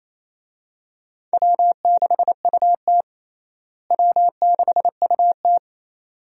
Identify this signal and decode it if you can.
Morse code, heard as a keyed tone, about 28 wpm, 700 Hz. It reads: W6UT W6UT
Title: USA Call Signs - 4 Characters - Repeated 2x - 28wpm